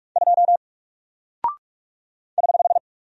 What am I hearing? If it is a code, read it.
Morse code, heard as a keyed tone, about 46 wpm, 700 Hz. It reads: 2 <HH>
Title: Single Character (Letter:Number:Punctuation:Prosign) Alt 45wpm